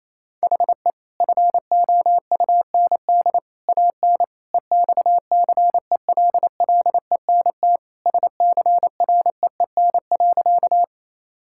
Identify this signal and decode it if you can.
Morse code, heard as a keyed tone, about 28 wpm, 700 Hz. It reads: HE FOUND AN EXCELLENT SCREEN.
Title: Sentences from Top 1000 Words - Review Part 6 - 28wpm